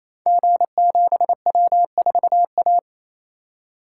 Morse code, heard as a keyed tone, about 28 wpm, 700 Hz. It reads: G7W4A
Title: Code Groups - 5 Letter:Numbers - 28wpm